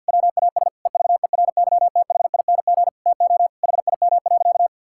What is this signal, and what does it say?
Morse code, heard as a keyed tone, about 50 wpm, 700 Hz. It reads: WAS EVERYTHING TO HIM.
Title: Sentences from Top 1600 Words 50wpm